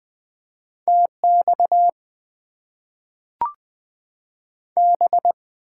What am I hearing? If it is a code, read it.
Morse code, heard as a keyed tone, about 20 wpm, 700 Hz. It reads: TX B
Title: Top 90 QSO Elements - 8x Character Spacing - 20wpm